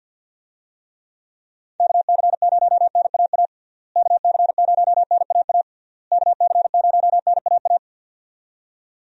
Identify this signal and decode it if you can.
Morse code, heard as a keyed tone, about 46 wpm, 700 Hz. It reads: KC0NAA KC0NAA KC0NAA
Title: USA Call Signs - 6 Characters - Repeated 3x - 50wpm